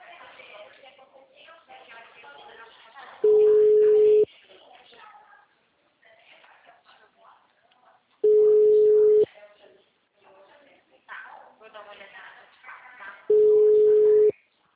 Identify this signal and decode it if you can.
Ringback tone (Europe, 425 Hz), with speech.
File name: out- -2009- - - .36522